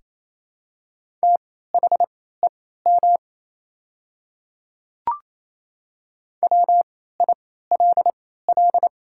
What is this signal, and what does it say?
Morse code, heard as a keyed tone, about 27 wpm, spaced as for 16 wpm, 700 Hz. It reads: THEM WILL
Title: 3x Character Spacing - Top 100 Words - 28wpm